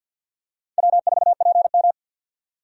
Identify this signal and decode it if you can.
Morse code, heard as a keyed tone, about 49 wpm, 700 Hz. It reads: W4PM
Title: USA Call Signs - 4 Characters - 50wpm